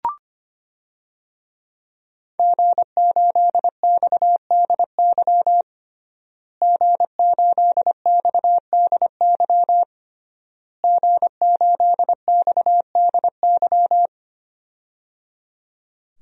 Morse code, heard as a keyed tone, about 25 wpm, 700 Hz. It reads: G8XDY G8XDY G8XDY
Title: DXCC-100 Call Signs - 5 Characters - Repeated 3x - 25wpm